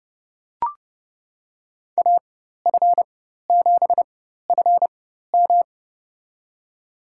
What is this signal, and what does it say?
Morse code, heard as a keyed tone, about 30 wpm, spaced as for 14 wpm, 700 Hz. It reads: AF7FM
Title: USA Call Signs - 5 Characters - 4x Character Spacing - 30wpm